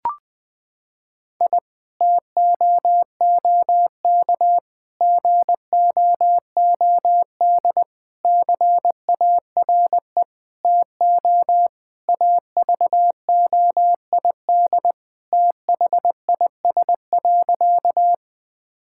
Morse code, heard as a keyed tone, about 20 wpm, 700 Hz. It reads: I TOOK GOOD CARE TO AVOID THIS.
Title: Sentences from Top 1000 Words - Review Part 6 - 20wpm